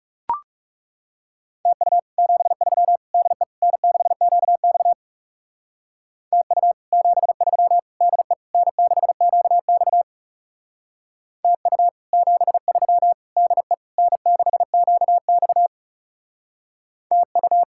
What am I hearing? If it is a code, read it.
Morse code, heard as a keyed tone, about 35 wpm, 700 Hz. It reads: TU 73 DE N6QX TU 73 DE N6QX TU 73 DE N6QX TU
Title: Speed-Racing Common QSO Phrases - 30wpm